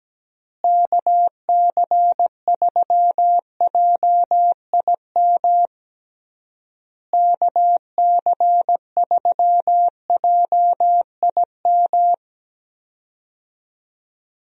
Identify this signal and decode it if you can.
Morse code, heard as a keyed tone, about 17 wpm, 700 Hz. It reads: KC3JIM KC3JIM
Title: USA Call Signs - 6 Characters - Repeated 3x - 17wpm